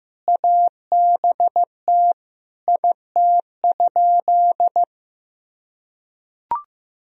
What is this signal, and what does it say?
Morse code, heard as a keyed tone, about 15 wpm, 700 Hz. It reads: ABT IT?
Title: Sentences from Top 100 Words and Abbreviations 15wpm